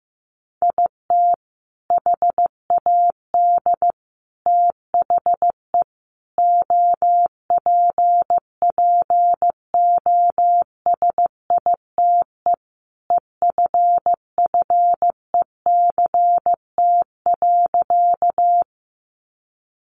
Morse code, heard as a keyed tone, about 15 wpm, 700 Hz. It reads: IT HAD THE OPPOSITE EFFECT.
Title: Sentences from Top 1300 Words 15wpm